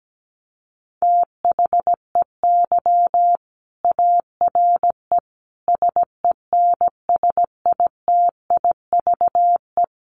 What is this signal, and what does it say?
Morse code, heard as a keyed tone, about 17 wpm, 700 Hz. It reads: THEY ARE SENSITIVE
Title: Sentences from Top 1400 Words 17wpm